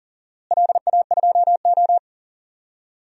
Morse code, heard as a keyed tone, about 40 wpm, 700 Hz. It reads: LA2O
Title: DXCC-100 Call Signs - 4 Characters - Repeated 2x - 40wpm